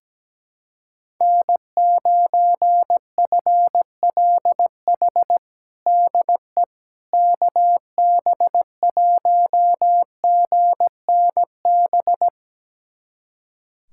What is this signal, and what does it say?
Morse code, heard as a keyed tone, about 17 wpm, 700 Hz. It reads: N9FLH DE KB1GNB
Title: USA Callsign DE Callsign 17wpm